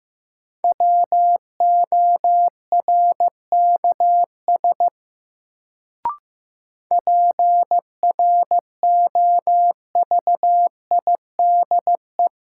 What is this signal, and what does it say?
Morse code, heard as a keyed tone, about 15 wpm, 700 Hz. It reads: WORKS PROVIDE
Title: Sentences from Top 1100 Words - New Word Drill Alt - 15wpm